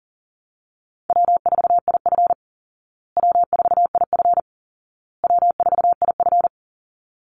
Morse code, heard as a keyed tone, about 40 wpm, 700 Hz. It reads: W4IF W4IF W4IF
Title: Call Signs Repeated 2x - 40wpm